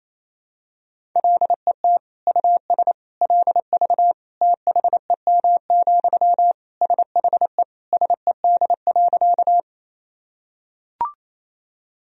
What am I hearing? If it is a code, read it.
Morse code, heard as a keyed tone, about 27 wpm, 700 Hz. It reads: LET US LV THEM, SHE SED.
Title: Sentences from Top 400 Words and Abbreviations 28wpm